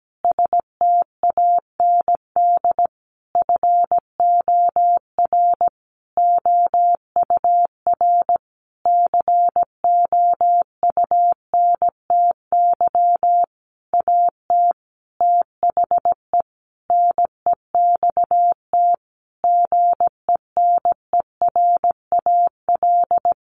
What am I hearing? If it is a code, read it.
Morse code, heard as a keyed tone, about 17 wpm, 700 Hz. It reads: STAND FOR OUR COUNTY AT THE NEXT GENERAL
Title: Sentences from Top 1500 Words 17wpm